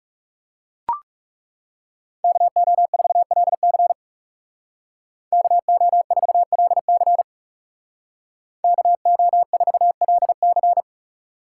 Morse code, heard as a keyed tone, about 36 wpm, 700 Hz. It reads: KO4LC KO4LC KO4LC
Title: Speed Racing - USA 2x2 Call Signs 30wpm